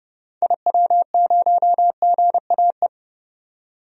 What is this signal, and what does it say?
Morse code, heard as a keyed tone, about 30 wpm, 700 Hz. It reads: IW0GAE
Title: DXCC-100 Call Signs - 6 Characters - 30wpm